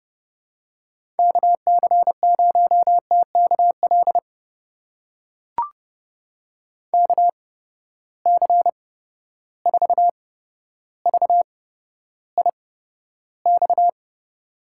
Morse code, heard as a keyed tone, about 29 wpm, spaced as for 13 wpm, 700 Hz. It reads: KC0TKL KC4VIX
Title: USA Call Signs - 6 Characters - 8x Character Spacing - 30wpm